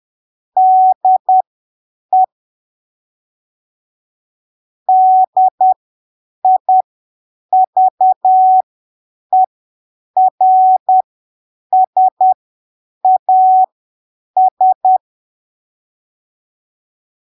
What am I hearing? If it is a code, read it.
Morse code, heard as a keyed tone, about 10 wpm, spaced as for 7 wpm, 750 Hz. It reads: DE DIVERSAS
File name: arquivos / texto / texto 06 10ppm